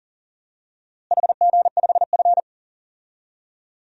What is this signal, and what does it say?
Morse code, heard as a keyed tone, about 40 wpm, 700 Hz. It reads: HG5F
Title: DXCC-100 Call Signs - 4 Characters - Repeated 3x - 40wpm